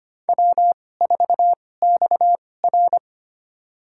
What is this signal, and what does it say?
Morse code, heard as a keyed tone, about 25 wpm, spaced as for 18 wpm, 700 Hz. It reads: W4XR
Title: USA Call Signs - 2x Character Spacing - 25wpm